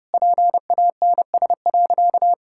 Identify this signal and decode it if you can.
Morse code, heard as a keyed tone, about 30 wpm, 700 Hz. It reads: PANS.
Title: Pangram Challenge v2 - 30wpm